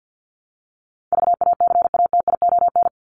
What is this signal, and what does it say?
Morse code, heard as a keyed tone, about 50 wpm, 700 Hz. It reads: VACATION
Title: Top 1501-1750 Words 50wpm